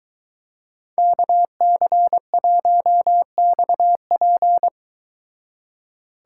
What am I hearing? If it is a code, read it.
Morse code, heard as a keyed tone, about 22 wpm, 700 Hz. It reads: KC1XP